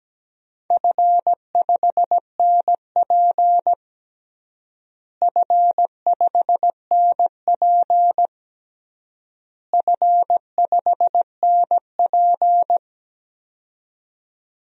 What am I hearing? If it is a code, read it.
Morse code, heard as a keyed tone, about 17 wpm, 700 Hz. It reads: F5NP F5NP F5NP
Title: DXCC-100 Call Signs - 4 Characters - Repeated 3x - 17wpm